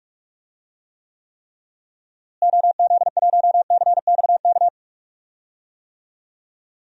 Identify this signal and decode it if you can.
Morse code, heard as a keyed tone, about 46 wpm, 700 Hz. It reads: OZ1CXK
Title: DXCC-100 Call Signs - 6 Characters - Repeated 3x - 45wpm